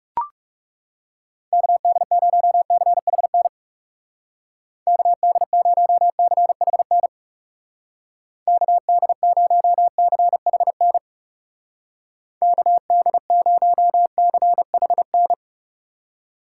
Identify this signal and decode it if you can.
Morse code, heard as a keyed tone, about 38 wpm, 700 Hz. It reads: KD0CHN KD0CHN KD0CHN KD0CHN